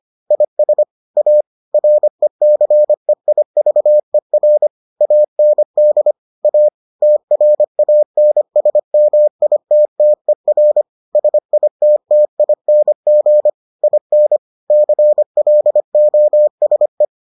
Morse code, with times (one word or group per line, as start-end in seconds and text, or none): 0.3-0.8 IS
1.2-1.4 A
1.7-4.7 RECEIVER
5.0-6.1 AND
6.4-6.7 A
7.0-10.8 TRANSMITTER
11.2-13.5 SITTING
13.8-14.4 IN
14.7-17.0 CLOSE